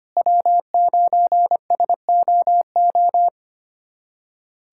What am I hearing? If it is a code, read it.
Morse code, heard as a keyed tone, about 25 wpm, 700 Hz. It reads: W9SOO
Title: USA Callsign DE Callsign - 5 Characters - 25wpm